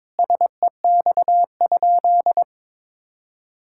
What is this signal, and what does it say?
Morse code, heard as a keyed tone, about 22 wpm, 700 Hz. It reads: SEX?
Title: Sentences from Top 1600 Words 22wpm